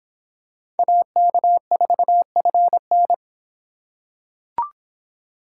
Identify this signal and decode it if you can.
Morse code, heard as a keyed tone, about 26 wpm, 700 Hz. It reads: AK4FN